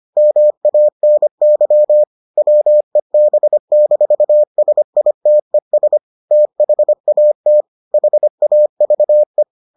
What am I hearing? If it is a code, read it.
Morse code, heard as a keyed tone, about 25 wpm, 600 Hz. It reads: MANY WEB-SITES THAT HAVE